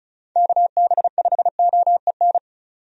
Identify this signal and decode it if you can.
Morse code, heard as a keyed tone, about 34 wpm, 700 Hz. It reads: KB5OEN
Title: USA Call Signs - 6 Characters - 35wpm